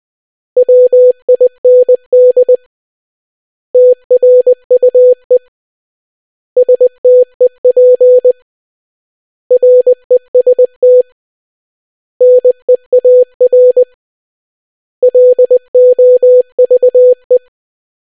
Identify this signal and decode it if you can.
Morse code, heard as a keyed tone, about 20 wpm, 500 Hz. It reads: WIND TRUE STEP REST NEAR LOVE